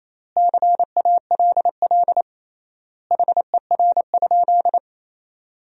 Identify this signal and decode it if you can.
Morse code, heard as a keyed tone, about 28 wpm, 700 Hz. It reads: CALL HER?
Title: 3x Word-Spacing - Sentences from Top 100 Words - 28wpm